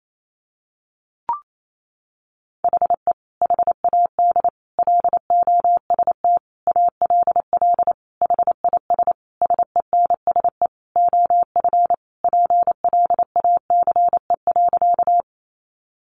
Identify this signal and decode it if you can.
Morse code, heard as a keyed tone, about 28 wpm, 700 Hz. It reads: HE HAD LOST ALL HIS SENSE OF PLACE.
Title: Sentences from Top 1400 Words 28wpm